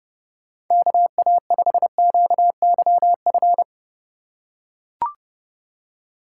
Morse code, heard as a keyed tone, about 30 wpm, 700 Hz. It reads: KA5QYF
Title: USA Call Signs - 6 Characters - 4x Character Spacing - 30wpm